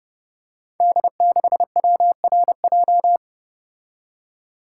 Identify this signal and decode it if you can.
Morse code, heard as a keyed tone, about 30 wpm, 700 Hz. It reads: D6WRJ
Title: Code Groups - 5 Letter:Numbers - 30wpm